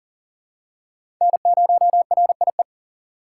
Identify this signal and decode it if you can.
Morse code, heard as a keyed tone, about 40 wpm, 700 Hz. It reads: N0RIE
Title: USA Call Signs - 5 Characters - 3x Character Spacing - 40wpm